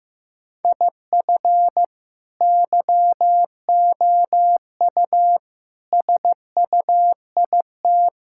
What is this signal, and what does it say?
Morse code, heard as a keyed tone, about 15 wpm, 700 Hz. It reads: IF YOU SUIT
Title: Sentences from Top 1600 Words 15wpm